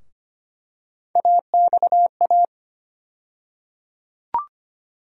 Morse code, heard as a keyed tone, about 25 wpm, 700 Hz. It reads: AXA
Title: USA Call Sign Postfixes - 3 Letters 25wpm